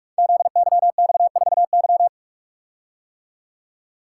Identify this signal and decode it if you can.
Morse code, heard as a keyed tone, about 44 wpm, 700 Hz. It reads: ZYX4Y
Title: Code Groups - 5 Letter:Numbers - 45wpm